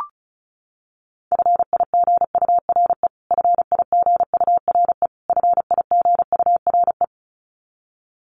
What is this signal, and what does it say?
Morse code, heard as a keyed tone, about 34 wpm, 700 Hz. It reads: FIGURE FIGURE FIGURE
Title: Top 500 Words Repeated 35wpm